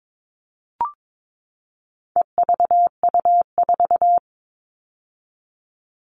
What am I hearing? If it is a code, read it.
Morse code, heard as a keyed tone, about 22 wpm, 700 Hz. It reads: EVU4
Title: All Cut Numbers - 4-Digit 22wpm